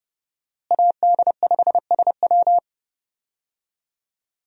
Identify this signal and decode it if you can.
Morse code, heard as a keyed tone, about 30 wpm, 700 Hz. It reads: AD5SW
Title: USA Call Signs - 5 Characters - Repeated 3x - 30wpm